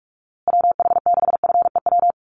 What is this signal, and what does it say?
Morse code, heard as a keyed tone, about 44 wpm, 700 Hz. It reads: WH6FEW
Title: USA Call Signs Repeated 3x 45wpm